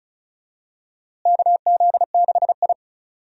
Morse code, heard as a keyed tone, about 34 wpm, 700 Hz. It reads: KZ6I
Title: USA Call Signs - 4 Characters - Repeated 3x - 35wpm